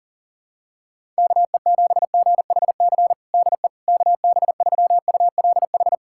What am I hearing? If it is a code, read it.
Morse code, heard as a keyed tone, about 40 wpm, 700 Hz. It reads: KE7GHC DE KB3ULH
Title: USA Callsign DE Callsign - 6 Characters - 40wpm